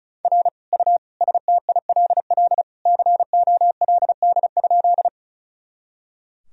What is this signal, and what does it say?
Morse code, heard as a keyed tone, about 34 wpm, 700 Hz. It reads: R U STILL COLD?
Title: Sentences from Top 400 Words and Abbreviations 35wpm